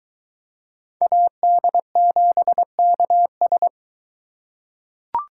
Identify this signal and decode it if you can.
Morse code, heard as a keyed tone, about 22 wpm, 700 Hz. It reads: AD7KS